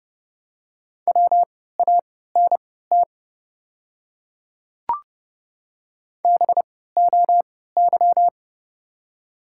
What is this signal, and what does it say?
Morse code, heard as a keyed tone, about 29 wpm, spaced as for 17 wpm, 700 Hz. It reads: WANT BOY
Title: 3x Character Spacing - Top 300 Words - 30wpm